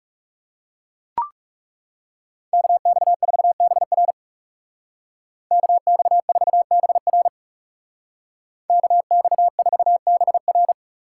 Morse code, heard as a keyed tone, about 39 wpm, 700 Hz. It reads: KX4BR KX4BR KX4BR